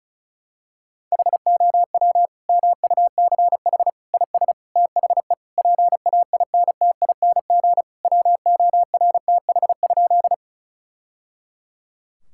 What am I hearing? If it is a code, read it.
Morse code, heard as a keyed tone, about 36 wpm, 690 Hz. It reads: HOW MUCH IS THE PAINTING WORTH?
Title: Sentences from Top 1000 Words - Review Part 6 - 35wpm